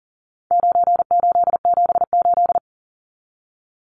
Morse code, heard as a keyed tone, about 40 wpm, 700 Hz. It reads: 9878
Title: Numbers - 4-Digit 40wpm